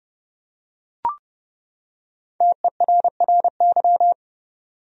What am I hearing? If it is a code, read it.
Morse code, heard as a keyed tone, about 30 wpm, 700 Hz. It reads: TERRY
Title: USA Common Names - 30wpm